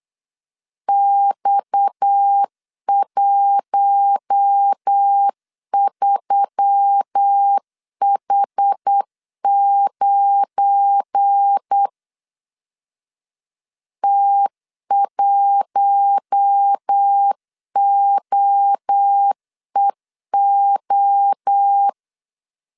Morse code, heard as a keyed tone, about 8 wpm, 800 Hz. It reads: X13H9 T1OEO
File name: LEZIONI / 8 YJ45 / 8 YJ45 8wpm 3214